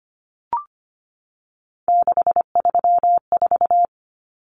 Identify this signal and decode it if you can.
Morse code, heard as a keyed tone, about 25 wpm, 700 Hz. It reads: 634
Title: Common Cut Numbers - 3-Digit 25wpm